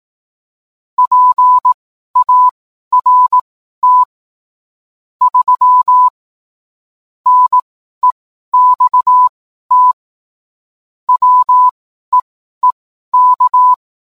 Morse code, heard as a keyed tone, about 18 wpm, spaced as for 12 wpm, 1000 Hz. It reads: PART 3 NEXT WEEK